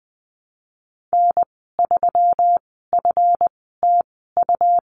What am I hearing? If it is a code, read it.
Morse code, heard as a keyed tone, about 20 wpm, spaced as for 15 wpm, 700 Hz. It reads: N3FTU
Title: 140 Lesson 23g - Letter f - Call Signs 20wpm